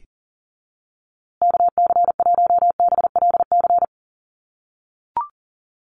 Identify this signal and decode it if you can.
Morse code, heard as a keyed tone, about 40 wpm, 700 Hz. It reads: KC1BLC
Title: USA Call Signs Repeated 3x 40wpm